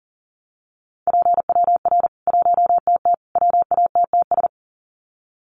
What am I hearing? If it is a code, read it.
Morse code, heard as a keyed tone, about 40 wpm, 700 Hz. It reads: PWR 1TT WATTS
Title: Common QSO Phrases 40wpm